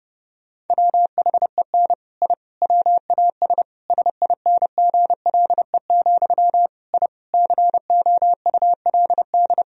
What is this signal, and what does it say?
Morse code, heard as a keyed tone, about 30 wpm, 700 Hz. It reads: WHEN I WAS SINGLE, I COULD
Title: Sentences from Top 1000 Words - Review Part 4 - 30wpm